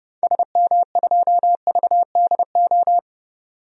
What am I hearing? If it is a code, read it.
Morse code, heard as a keyed tone, about 30 wpm, 700 Hz. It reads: SM2VDO
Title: DXCC-100 Call Signs - 6 Characters - 30wpm